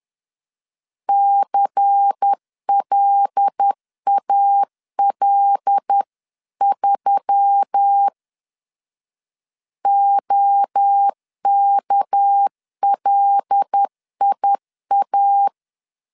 Morse code, heard as a keyed tone, about 10 wpm, 800 Hz. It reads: CLAL3 OKLIA